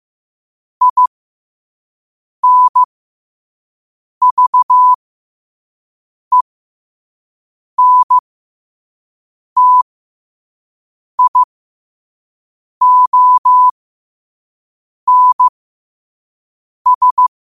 Morse code, heard as a keyed tone, about 15 wpm, spaced as for 5 wpm, 1000 Hz. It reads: INVENTIONS
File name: qotdfiles / 5wpm